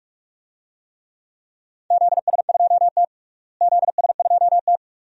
Morse code, heard as a keyed tone, about 44 wpm, 700 Hz. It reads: ZS2T ZS2T
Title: DXCC-100 Call Signs - 4 Characters - Repeated 2x - 45wpm